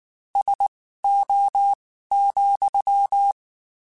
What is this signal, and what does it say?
Morse code, heard as a keyed tone, about 19 wpm, spaced as for 14 wpm, 800 Hz. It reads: SO,
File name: (spaced) / QSOs Spaced-mp3s / WD0BGP 1 @18 sp